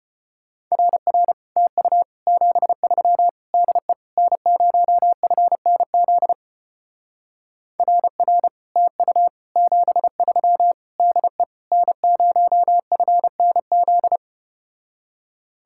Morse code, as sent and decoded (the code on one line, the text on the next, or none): .-. .-. / - ..- / --... ...-- / -.. / -. ----- ..-. -. --.. / .-. .-. / - ..- / --... ...-- / -.. / -. ----- ..-. -. --..
RR TU 73 DE N0FNZ RR TU 73 DE N0FNZ